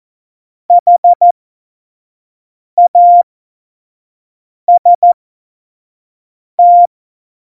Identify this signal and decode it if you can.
Morse code, heard as a keyed tone, about 14 wpm, spaced as for 5 wpm, 700 Hz. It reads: HAST
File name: Morsekurs / Witze / Witz 07. November 2017, 25 F70